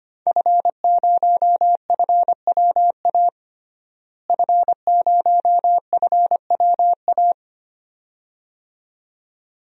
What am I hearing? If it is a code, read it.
Morse code, heard as a keyed tone, about 25 wpm, 700 Hz. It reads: F0FWA F0FWA
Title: DXCC-100 Call Signs - 5 Characters - Repeated 2x - 25wpm